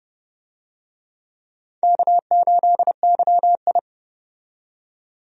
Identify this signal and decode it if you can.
Morse code, heard as a keyed tone, about 32 wpm, 700 Hz. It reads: K8YI
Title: USA Call Signs - 4 Characters - 30wpm